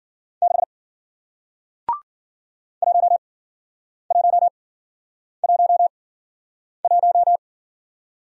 Morse code, heard as a keyed tone, about 48 wpm, spaced as for 9 wpm, 700 Hz. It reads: 6 1111
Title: Speed-Racing Single Letter-Number 40wpm